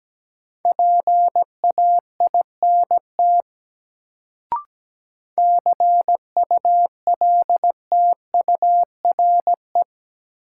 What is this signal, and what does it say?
Morse code, heard as a keyed tone, about 17 wpm, 700 Hz. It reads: PAINT CULTURE
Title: Sentences from Top 1000 Words - Review Part 6 - New Word Drill - 17wpm